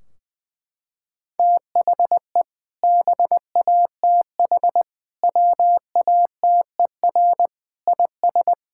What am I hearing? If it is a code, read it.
Morse code, heard as a keyed tone, about 21 wpm, 700 Hz. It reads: THE BATH WATER IS
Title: Sentences from Top 1600 Words 20wpm